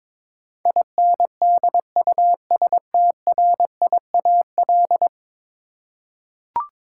Morse code, heard as a keyed tone, about 22 wpm, 700 Hz. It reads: INDUSTRIAL
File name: practice / 22wpm / 10 Letter Words - First 3 Letters Given - 22wpm